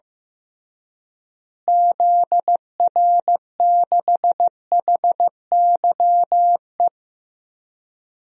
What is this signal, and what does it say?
Morse code, heard as a keyed tone, about 15 wpm, 700 Hz. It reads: ZR6HYE
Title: DXCC-100 Call Signs - 6 Characters - Repeated 2x - 15wpm